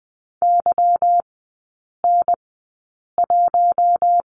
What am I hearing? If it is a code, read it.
Morse code, heard as a keyed tone, about 20 wpm, spaced as for 8 wpm, 690 Hz. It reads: YN1
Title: 145 Lesson 24e - Letter y - Three Characters 20wpm